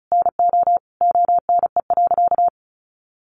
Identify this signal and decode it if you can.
Morse code, heard as a keyed tone, about 34 wpm, 700 Hz. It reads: NO ONE.
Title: Sentences from Top 1400 Words 35wpm